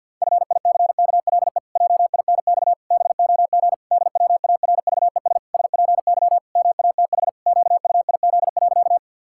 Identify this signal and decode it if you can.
Morse code, heard as a keyed tone, about 48 wpm, 700 Hz. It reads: FICKLE JINX BOG DWARVES SPY MATH QUIZ.
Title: Pangram Challenge v2 - 50wpm